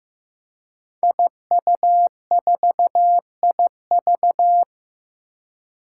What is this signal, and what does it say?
Morse code, heard as a keyed tone, about 15 wpm, 700 Hz. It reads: IU4IV
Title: Five Challenging Letter-Number Combinations 15wpm